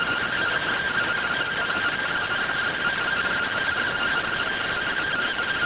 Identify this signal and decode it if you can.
Two-tone teleprinter signal, about 170 Hz shift.